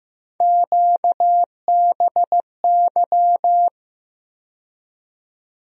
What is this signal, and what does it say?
Morse code, heard as a keyed tone, about 15 wpm, 700 Hz. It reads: QBY
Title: USA Call Sign Postfixes - 3 Letters 15wpm